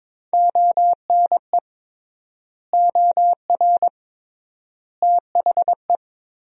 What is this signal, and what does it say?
Morse code, heard as a keyed tone, about 22 wpm, 700 Hz. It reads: ONE OR THE
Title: 3x Word-Spacing - Sentences from Top 400 Words - 22wpm